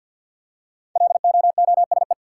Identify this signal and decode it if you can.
Morse code, heard as a keyed tone, about 53 wpm, 700 Hz. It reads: LOOSE
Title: Five Letter Words - 50wpm